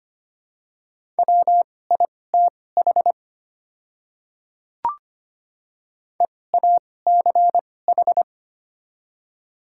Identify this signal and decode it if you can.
Morse code, heard as a keyed tone, about 25 wpm, spaced as for 18 wpm, 690 Hz. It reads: WITH EACH
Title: 2x Character Spacing - Top 100 Words - 25wpm